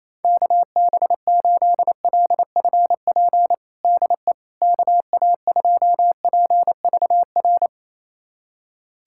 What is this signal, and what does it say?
Morse code, heard as a keyed tone, about 28 wpm, 700 Hz. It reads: KB8LFP DE KA2PVR
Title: USA Callsign DE Callsign - 6 Characters - 28wpm